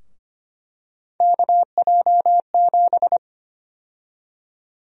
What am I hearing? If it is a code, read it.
Morse code, heard as a keyed tone, about 25 wpm, 700 Hz. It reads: KJ7
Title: USA Call Sign Prefixes - Repeated 2x - 25wpm